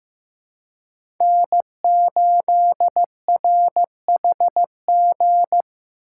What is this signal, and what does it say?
Morse code, heard as a keyed tone, about 15 wpm, 700 Hz. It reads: N8RHG
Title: USA Call Signs - 5 Characters - 4x Character Spacing - 15wpm